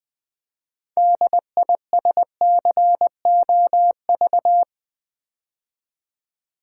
Morse code, heard as a keyed tone, about 20 wpm, 700 Hz. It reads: DISCOV
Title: Build A Word - 3 Letters At A Time - 8-Letter Words - 20wpm